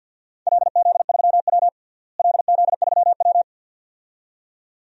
Morse code, heard as a keyed tone, about 54 wpm, 700 Hz. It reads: LZ3W LZ3W
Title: DXCC-100 Call Signs - 4 Characters - Repeated 2x - 50wpm